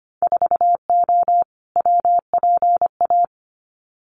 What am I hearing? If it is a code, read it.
Morse code, heard as a keyed tone, about 25 wpm, 700 Hz. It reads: 4O WPA
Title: Winter Field Day Exchanges 25wpm